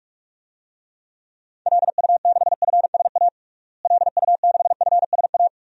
Morse code, heard as a keyed tone, about 46 wpm, 700 Hz. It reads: LU6FSA LU6FSA
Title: DXCC-100 Call Signs - 6 Characters - Repeated 3x - 45wpm